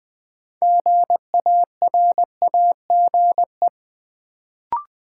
Morse code, heard as a keyed tone, about 20 wpm, 700 Hz. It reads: GARAGE
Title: Build A Word - 3 Letters At A Time - 6-Letter Words - 20wpm